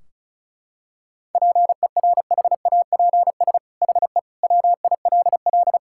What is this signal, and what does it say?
Morse code, heard as a keyed tone, about 34 wpm, 700 Hz. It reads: PERHAPS HE WILL
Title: Sentences from Top 1600 Words 35wpm